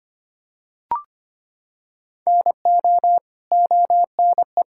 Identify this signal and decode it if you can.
Morse code, heard as a keyed tone, about 25 wpm, 700 Hz. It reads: NO ONE
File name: practice / 25wpm / Sentences with 2-Word Phrases - 25wpm